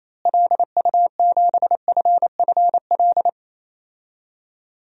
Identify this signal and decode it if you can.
Morse code, heard as a keyed tone, about 27 wpm, 700 Hz. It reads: LU7FFL
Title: DXCC-100 Call Signs - 6 Characters - Repeated 3x - 28wpm